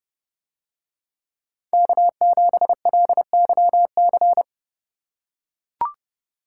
Morse code, heard as a keyed tone, about 31 wpm, 700 Hz. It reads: K7LYC